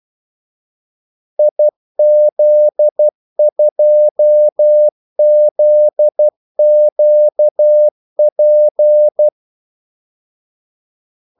Morse code, heard as a keyed tone, about 12 wpm, 600 Hz. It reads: IZ2ZQP
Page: 2 ekstra ordmellemrum mellem kaldesignalerne